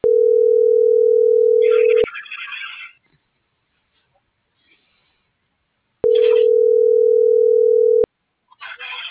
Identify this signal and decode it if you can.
Ringback tone (North America), 440 + 480 Hz.